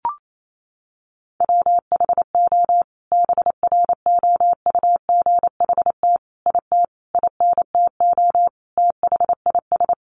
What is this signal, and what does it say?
Morse code, heard as a keyed tone, about 28 wpm, 700 Hz. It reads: WHO BROUGHT IT INTO THIS
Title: Sentences from Top 600 Words 28wpm